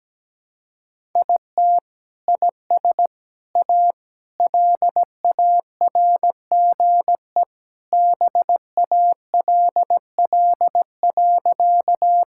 Morse code, heard as a keyed tone, about 17 wpm, 700 Hz. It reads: IT IS A LARGE BALL.
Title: Sentences from Top 1000 Words - Review Part 6 - 17wpm